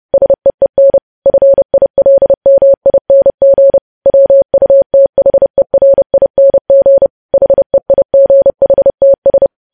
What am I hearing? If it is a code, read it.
Morse code, heard as a keyed tone, about 30 wpm, 550 Hz. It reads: SEEN FILMING WUTHERING HEIGHTS
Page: news headlines in morse code at 30 wpm